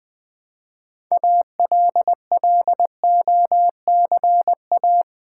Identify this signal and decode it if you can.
Morse code, heard as a keyed tone, about 20 wpm, 700 Hz. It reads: ALLOCA
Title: Build A Word - 3 Letters At A Time - 8-Letter Words - 20wpm